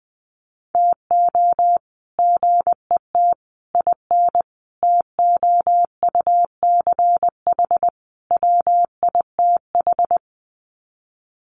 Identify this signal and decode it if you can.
Morse code, heard as a keyed tone, about 20 wpm, 700 Hz. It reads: TO GET IN TOUCH WITH
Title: Sets of 5 Words Encore2 20wpm